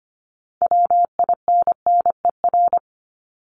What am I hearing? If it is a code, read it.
Morse code, heard as a keyed tone, about 25 wpm, 700 Hz. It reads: WINNER
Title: Top 1751-2000 Words 25wpm